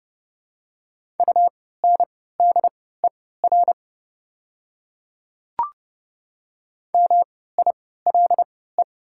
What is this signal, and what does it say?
Morse code, heard as a keyed tone, about 30 wpm, spaced as for 17 wpm, 700 Hz. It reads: UNDER MILE
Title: 3x Character Spacing - Top 300 Words - 30wpm